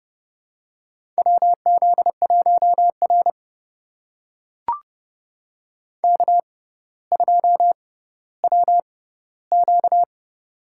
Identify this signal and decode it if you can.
Morse code, heard as a keyed tone, about 30 wpm, spaced as for 15 wpm, 700 Hz. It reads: WZ1R K2WQ
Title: USA Call Signs - 4 Characters - 6x Character Spacing - 30wpm